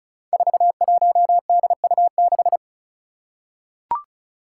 Morse code, heard as a keyed tone, about 36 wpm, 700 Hz. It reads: 41DU6
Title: Five Challenging Letter-Number Combinations 35wpm